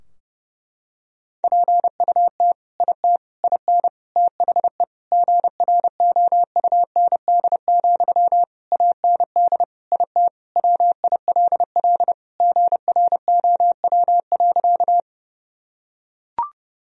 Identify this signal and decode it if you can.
Morse code, heard as a keyed tone, about 30 wpm, 700 Hz. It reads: PUT IT IN THE GROUND, AND IT WILL GROW.
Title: Sentences from Top 1000 Words - Review Part 2 - 30wpm